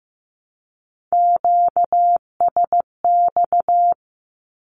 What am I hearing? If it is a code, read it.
Morse code, heard as a keyed tone, about 15 wpm, 700 Hz. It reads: QSX
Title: Q-Codes Repeated 3x 15wpm